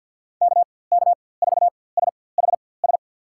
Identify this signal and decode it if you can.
Morse code, heard as a keyed tone, about 48 wpm, spaced as for 23 wpm, 700 Hz. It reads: KK4SHS
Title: USA Call Signs - 6 Characters - 4x Character Spacing - 50wpm